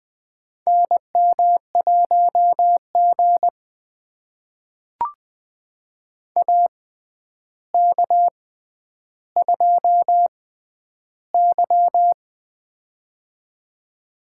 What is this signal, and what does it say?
Morse code, heard as a keyed tone, about 20 wpm, spaced as for 10 wpm, 700 Hz. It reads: NM1G AK2Y
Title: USA Call Signs - 4 Characters - 6x Character Spacing - 20wpm